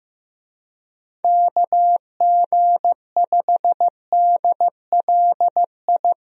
Morse code, heard as a keyed tone, about 15 wpm, 700 Hz. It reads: KG5DLI
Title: USA Call Signs - 6 Characters - Repeated 3x - 15wpm